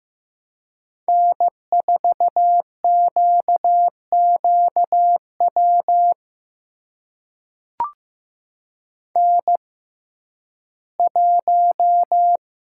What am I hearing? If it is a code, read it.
Morse code, heard as a keyed tone, about 15 wpm, spaced as for 11 wpm, 700 Hz. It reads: N4QQW N1
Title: USA Call Signs - 5 Characters - 6x Character Spacing - 15wpm